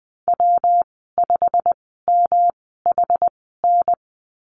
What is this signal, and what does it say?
Morse code, heard as a keyed tone, about 19 wpm, spaced as for 14 wpm, 700 Hz. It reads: W5MHN
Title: 173 Lesson 28g - Number 9 - Call Signs 20wpm